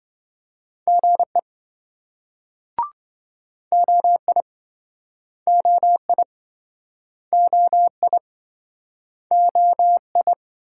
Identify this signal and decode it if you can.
Morse code, heard as a keyed tone, about 27 wpm, 700 Hz. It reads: GE OI OI OI OI